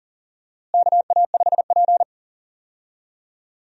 Morse code, heard as a keyed tone, about 40 wpm, 700 Hz. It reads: KA5P
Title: USA Call Signs - 4 Characters - Repeated 2x - 40wpm